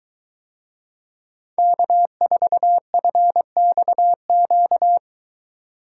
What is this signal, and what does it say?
Morse code, heard as a keyed tone, about 23 wpm, 700 Hz. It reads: K4FXQ